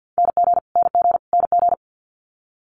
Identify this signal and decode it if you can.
Morse code, heard as a keyed tone, about 50 wpm, 700 Hz. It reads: NG NG NG
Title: Multi-letter Phonemes Repeated 3x 50wpm